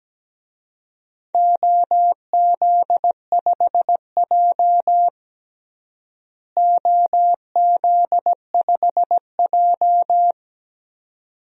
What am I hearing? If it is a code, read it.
Morse code, heard as a keyed tone, about 17 wpm, 700 Hz. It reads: OZ5J OZ5J
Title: DXCC-100 Call Signs - 4 Characters - Repeated 2x - 17wpm